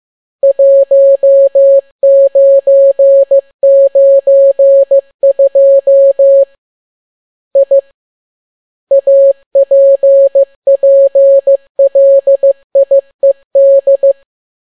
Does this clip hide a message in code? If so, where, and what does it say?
Yes: Morse code from 0.4 to 14.1 s, heard as a keyed tone, about 15 wpm, 550 Hz. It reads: 1992 I APPLIED